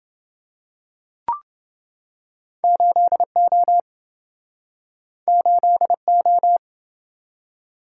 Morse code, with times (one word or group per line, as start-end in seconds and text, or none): 2.6-3.8 8O
5.3-6.6 8O